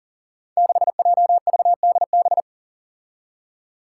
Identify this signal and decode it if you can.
Morse code, heard as a keyed tone, about 41 wpm, 700 Hz. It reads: 6JVDB
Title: Five Challenging Letter-Number Combinations 40wpm